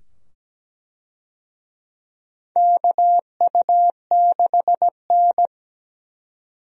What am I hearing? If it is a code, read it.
Morse code, heard as a keyed tone, about 17 wpm, 700 Hz. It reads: KU6N
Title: USA Call Signs - 4 Characters - 17wpm